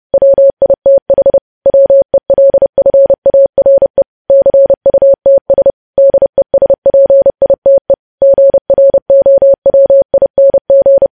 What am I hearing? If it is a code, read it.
Morse code, heard as a keyed tone, about 30 wpm, 550 Hz. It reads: WITH WELFARE CUTS DESPITE GROWING
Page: News Headlines in Morse Code at 30 WPM